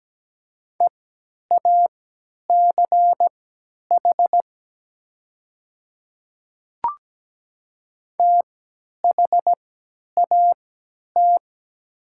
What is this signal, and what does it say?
Morse code, heard as a keyed tone, about 17 wpm, spaced as for 10 wpm, 700 Hz. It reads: EACH THAT
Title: 3x Character Spacing - Top 100 Words - 17wpm